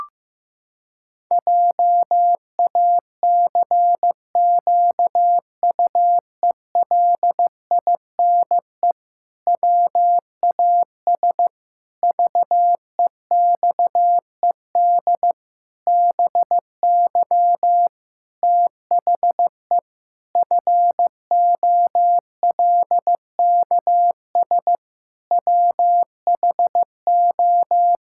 Morse code, heard as a keyed tone, about 15 wpm, 700 Hz. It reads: JACQUELINE WAS VEXED BY THE FOLKS WHO
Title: Pangram Challenge v2 - 15wpm